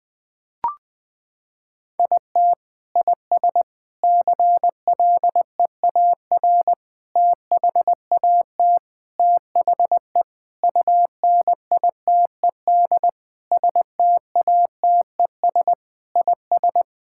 Morse code, heard as a keyed tone, about 20 wpm, 700 Hz. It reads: IT IS CLEAR THAT THE UNITED STATES IS
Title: Sentences with 4-Word Phrases - 20wpm